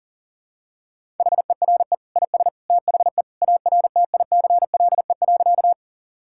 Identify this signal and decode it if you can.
Morse code, heard as a keyed tone, about 40 wpm, 700 Hz. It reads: HERE IS THE ARTICLE.
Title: Sentences from Top 1000 Words - Review Part 5 - 40wpm